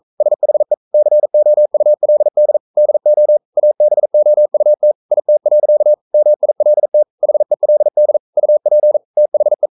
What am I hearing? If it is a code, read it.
Morse code, heard as a keyed tone, about 42 wpm, 600 Hz. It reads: SHE COULD DO ABOUT IT. MILT HELD UP THE
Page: Romanen Never af Ken Follet på engelsk.